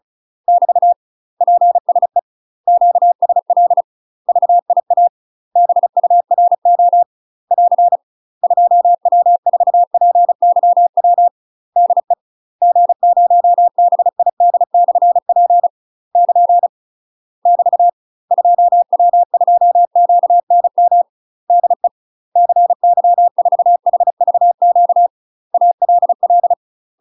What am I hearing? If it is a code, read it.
Morse code, heard as a keyed tone, about 36 wpm, 700 Hz. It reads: <BT> PSE QSL VIA BURO <AR> 2W4PYW DE G0BID/P <KN> <BT> 2W2QNM DE CY4HVQ ALL
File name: UK MASTER QSOs TRAINING / 35WPM / UK MASTER QSOs 0031